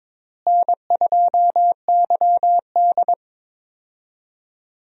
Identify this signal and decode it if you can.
Morse code, heard as a keyed tone, about 21 wpm, 700 Hz. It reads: N2YD
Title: USA Call Signs - 4 Characters - Repeated 3x - 22wpm